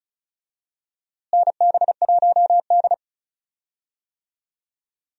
Morse code, heard as a keyed tone, about 34 wpm, 700 Hz. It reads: NB1D
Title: USA Call Signs - 4 Characters - Repeated 3x - 35wpm